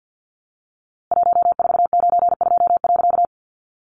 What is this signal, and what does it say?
Morse code, heard as a keyed tone, about 50 wpm, 700 Hz. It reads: 1492.
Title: A Sound Of Thunder 50wpm